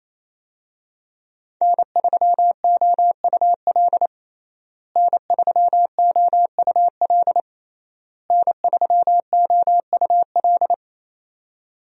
Morse code, heard as a keyed tone, about 28 wpm, 700 Hz. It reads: N3OUL N3OUL N3OUL
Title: USA Call Signs - 5 Characters - Repeated 3x - 28wpm